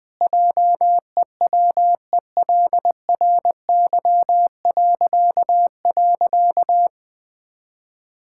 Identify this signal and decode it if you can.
Morse code, heard as a keyed tone, about 20 wpm, 700 Hz. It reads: JEWELRY..
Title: Pangram Challenge v2 - 20wpm